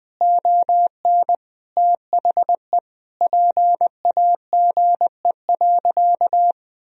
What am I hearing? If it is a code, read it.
Morse code, heard as a keyed tone, about 21 wpm, 700 Hz. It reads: ON THE PAGE.
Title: Sentences from Top 1000 Words - Review Part 1 - 20wpm